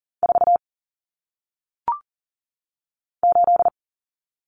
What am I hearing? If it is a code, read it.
Morse code, heard as a keyed tone, about 39 wpm, 700 Hz. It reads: 4 8
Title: Numbers - 1-Digit 40wpm